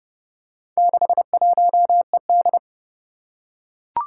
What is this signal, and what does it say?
Morse code, heard as a keyed tone, about 28 wpm, 700 Hz. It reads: 61ED
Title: Four Challenging Letter-Number Combinations 30wpm